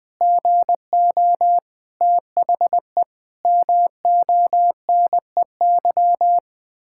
Morse code, heard as a keyed tone, about 20 wpm, 700 Hz. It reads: GO THE MONEY
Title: Pangram Challenge v2 - 20wpm